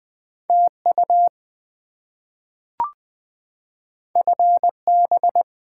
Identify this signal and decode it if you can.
Morse code, heard as a keyed tone, about 20 wpm, 700 Hz. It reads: TU FB
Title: Top 20 QSO Elements - 20wpm